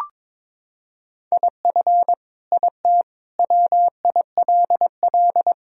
Morse code, heard as a keyed tone, about 22 wpm, 700 Hz. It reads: IF IT WILL
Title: Sentences from Top 1000 Words - Review Part 2 - 22wpm